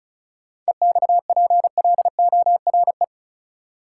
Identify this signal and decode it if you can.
Morse code, heard as a keyed tone, about 36 wpm, 700 Hz. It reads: EXPLORE
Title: Sentences from Top 1100 Words - New Word Drill Alt - 35wpm